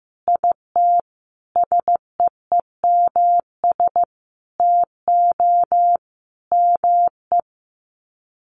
Morse code, heard as a keyed tone, about 15 wpm, 700 Hz. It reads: IT SEEMS TO ME
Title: Sets of 5 Words Encore2 15wpm